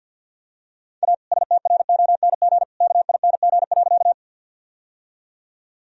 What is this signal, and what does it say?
Morse code, heard as a keyed tone, about 49 wpm, 700 Hz. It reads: A STRONG KING.
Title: Sentences from Top 1000 Words - Review Part 3 - 50wpm